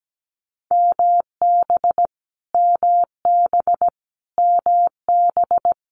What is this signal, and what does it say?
Morse code, heard as a keyed tone, about 17 wpm, 700 Hz. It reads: MB MB MB
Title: Multi-letter Phonemes Repeated 3x 17wpm